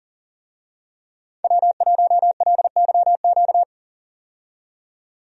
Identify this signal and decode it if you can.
Morse code, heard as a keyed tone, about 40 wpm, 700 Hz. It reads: W1LYQ
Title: USA Call Signs - 5 Characters - 40wpm